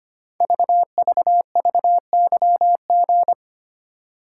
Morse code, heard as a keyed tone, about 25 wpm, 700 Hz. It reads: VVVYG
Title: Code Groups - 5 Letters - 25wpm